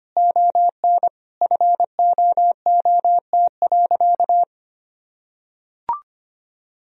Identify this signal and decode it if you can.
Morse code, heard as a keyed tone, about 25 wpm, 700 Hz. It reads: ON FOOT.
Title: Sentences from Top 1000 Words - Review Part 3 - 25wpm